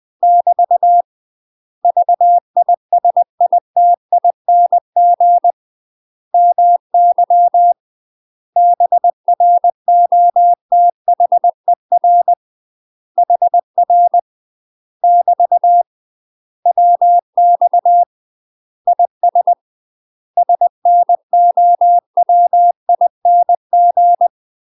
Morse code, heard as a keyed tone, about 20 wpm, 700 Hz. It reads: <BT> VISITING MY BROTHER HR <BT> WX IS SNOWING
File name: UK MASTER QSOs TRAINING/20WPM